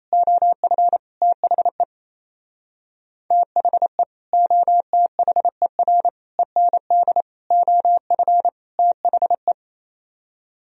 Morse code, heard as a keyed tone, about 35 wpm, 700 Hz. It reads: OF THE THE OTHER END OF THE